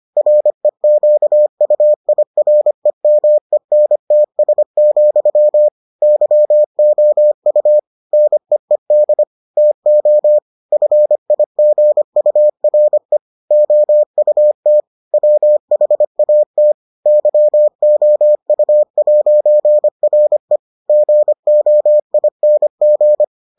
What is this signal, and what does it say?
Morse code, heard as a keyed tone, about 25 wpm, 600 Hz. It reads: REQUIREMENTS, YOU NEED TO FIGURE OUT WHAT YOU'RE GOING